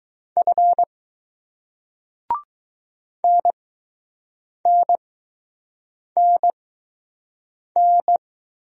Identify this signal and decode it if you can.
Morse code, heard as a keyed tone, about 20 wpm, spaced as for 6 wpm, 700 Hz. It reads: F NNNN